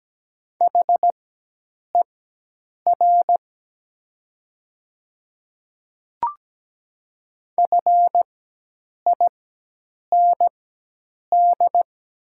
Morse code, heard as a keyed tone, about 17 wpm, spaced as for 8 wpm, 700 Hz. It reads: HER FIND
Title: 4x Character Spacing - Top 100 Words - 17wpm